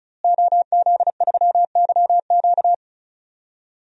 Morse code, heard as a keyed tone, about 36 wpm, 700 Hz. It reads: OZ3YQ
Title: DXCC-100 Call Signs - 5 Characters - Repeated 2x - 35wpm